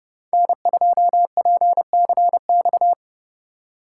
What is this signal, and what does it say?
Morse code, heard as a keyed tone, about 28 wpm, 700 Hz. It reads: N2PCX